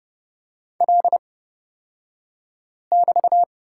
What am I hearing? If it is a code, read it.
Morse code, heard as a keyed tone, about 29 wpm, spaced as for 5 wpm, 700 Hz. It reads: L<BT>
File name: Lesson 36d - BT - Two Characters 30wpm